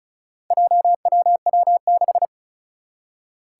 Morse code, heard as a keyed tone, about 36 wpm, 700 Hz. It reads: JWW6
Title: Four Challenging Letter-Number Combinations 35wpm